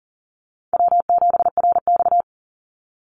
Morse code, heard as a keyed tone, about 40 wpm, 700 Hz. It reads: W7RX
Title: Easy Call Signs Repeated 3x - 40wpm